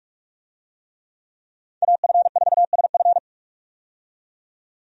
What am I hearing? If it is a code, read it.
Morse code, heard as a keyed tone, about 44 wpm, 700 Hz. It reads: AF4SF
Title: USA Call Signs - 5 Characters - 45wpm